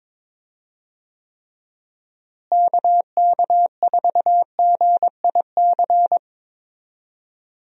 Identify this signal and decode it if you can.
Morse code, heard as a keyed tone, about 22 wpm, 700 Hz. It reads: KK4GIC
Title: USA Call Signs - 6 Characters - Repeated 2x - 22wpm